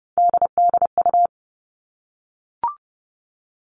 Morse code, heard as a keyed tone, about 30 wpm, 700 Hz. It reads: DDU
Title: All Cut Numbers - 3-Digit 30wpm